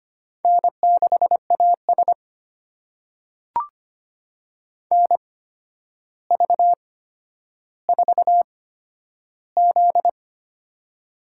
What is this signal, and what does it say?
Morse code, heard as a keyed tone, about 25 wpm, spaced as for 11 wpm, 700 Hz. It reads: N6AS NV4Z
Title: USA Call Signs - 4 Characters - 8x Character Spacing - 25wpm